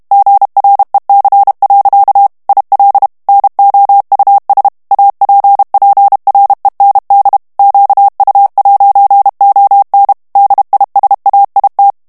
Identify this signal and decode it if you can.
Morse code, heard as a keyed tone, about 32 wpm, 800 Hz. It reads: GREC. IL NOUS APPREND QU'ON DISAIT
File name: FRANCAIS TEXTE CLAIR 2 VITESSE 30 MOTS MINUTES